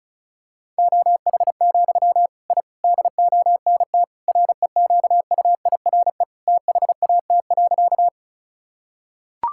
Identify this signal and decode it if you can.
Morse code, heard as a keyed tone, about 36 wpm, 700 Hz. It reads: OH, I DONT REQUIRE THAT.
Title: Sentences from Top 1000 Words - Review Part 6 - 35wpm